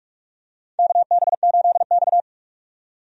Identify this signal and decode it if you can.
Morse code, heard as a keyed tone, about 44 wpm, 700 Hz. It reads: KB8X
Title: USA Call Signs - 4 Characters - 45wpm